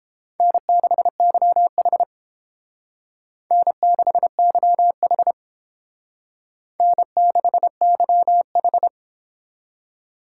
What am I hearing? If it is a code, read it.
Morse code, heard as a keyed tone, about 29 wpm, 700 Hz. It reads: N6YH N6YH N6YH